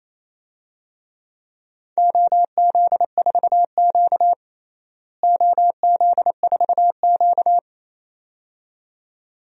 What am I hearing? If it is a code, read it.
Morse code, heard as a keyed tone, about 28 wpm, 700 Hz. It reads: OZ4Q OZ4Q
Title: DXCC-100 Call Signs - 4 Characters - Repeated 2x - 28wpm